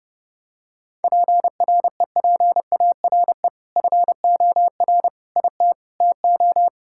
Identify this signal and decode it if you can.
Morse code, heard as a keyed tone, about 30 wpm, 700 Hz. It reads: PREPARE FOR IT TO
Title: Sentences from Top 1000 Words - Review Part 6 - 30wpm